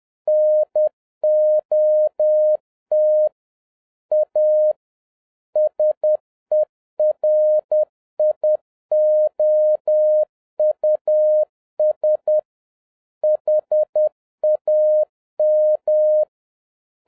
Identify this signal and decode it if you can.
Morse code, heard as a keyed tone, about 10 wpm, 610 Hz. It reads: NOT A SERIOUS HAM